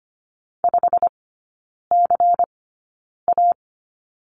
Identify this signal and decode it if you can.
Morse code, heard as a keyed tone, about 25 wpm, spaced as for 9 wpm, 700 Hz. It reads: 5CA
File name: course / 25:10wpm / 093 Lesson 16e - Letter c - Three Characters 25wpm